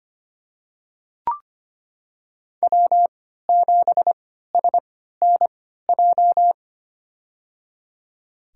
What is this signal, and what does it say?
Morse code, heard as a keyed tone, about 25 wpm, spaced as for 14 wpm, 700 Hz. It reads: W7SNJ
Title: USA Call Signs - 3x Character Spacing - 25wpm